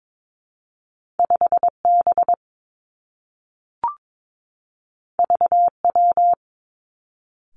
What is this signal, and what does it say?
Morse code, heard as a keyed tone, about 22 wpm, 700 Hz. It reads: 5 B V W
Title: Two Challenging Letter-Number Combinations 22wpm